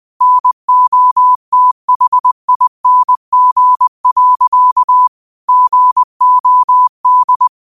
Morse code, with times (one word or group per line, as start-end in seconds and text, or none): 0.2-5.1 NOTHING.
5.5-7.5 GOD